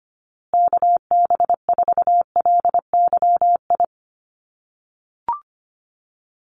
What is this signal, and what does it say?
Morse code, heard as a keyed tone, about 25 wpm, 700 Hz. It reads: KB4LYI
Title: USA Call Signs Repeated 2x 25wpm